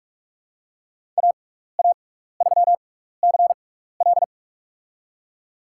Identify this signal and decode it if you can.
Morse code, heard as a keyed tone, about 45 wpm, spaced as for 15 wpm, 700 Hz. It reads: AA3CL
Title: USA Call Signs - 6x Character Spacing - 45wpm